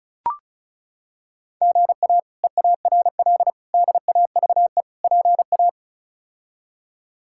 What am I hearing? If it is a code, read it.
Morse code, heard as a keyed tone, about 36 wpm, 700 Hz. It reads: GA EARL DAVE PA
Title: K1USN SST QSO Practice - 35wpm